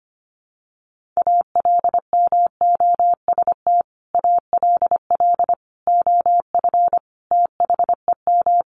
Morse code, heard as a keyed tone, about 25 wpm, 700 Hz. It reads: ALMOST ALL OF THEM
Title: Sentences from Top 1500 Words 25wpm